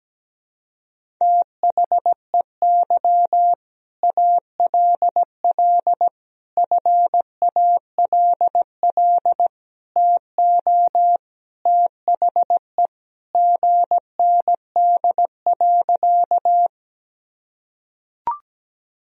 Morse code, heard as a keyed tone, about 17 wpm, 700 Hz. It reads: THEY ALL FALL TO THE GND.
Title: Sentences from Top 500 Words and Abbreviations 17wpm